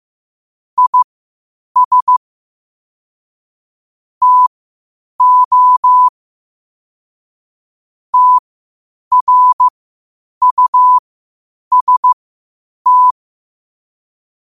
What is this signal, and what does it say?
Morse code, heard as a keyed tone, about 15 wpm, spaced as for 8 wpm, 1000 Hz. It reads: IS TO TRUST
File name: qotdfiles / 8wpm